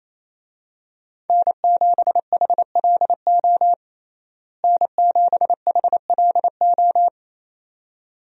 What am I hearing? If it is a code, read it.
Morse code, heard as a keyed tone, about 27 wpm, 700 Hz. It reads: N7HLO N7HLO
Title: USA Call Signs - 5 Characters - Repeated 2x - 28wpm